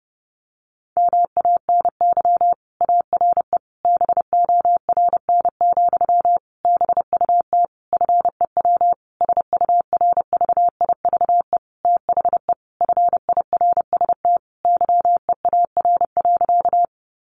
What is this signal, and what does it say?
Morse code, heard as a keyed tone, about 28 wpm, 700 Hz. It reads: MANY ARE BORN, BUT FEW SURVIVE THE FIRST YEAR.
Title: Sentences from Top 1300 Words 30wpm